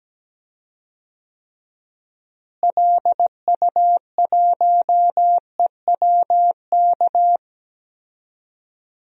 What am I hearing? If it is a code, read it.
Morse code, heard as a keyed tone, about 17 wpm, 700 Hz. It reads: LU1EWK
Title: DXCC-100 Call Signs - 6 Characters - Repeated 3x - 17wpm